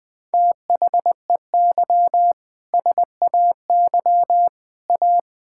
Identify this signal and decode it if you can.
Morse code, heard as a keyed tone, about 20 wpm, 700 Hz. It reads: THEY SAY A
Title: Sentences with 7-Word Phrases - 20wpm